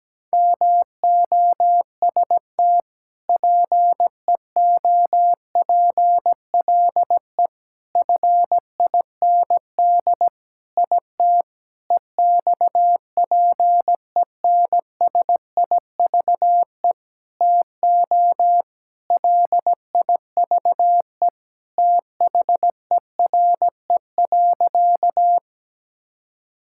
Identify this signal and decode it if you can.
Morse code, heard as a keyed tone, about 17 wpm, 700 Hz. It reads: MOST PEOPLE FIND IT EXPENSIVE TO LIVE THERE.
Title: Sentences from Top 1000 Words - Review Part 6 - 17wpm